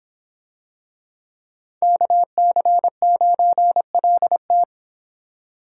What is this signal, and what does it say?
Morse code, heard as a keyed tone, about 25 wpm, 700 Hz. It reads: KC9LT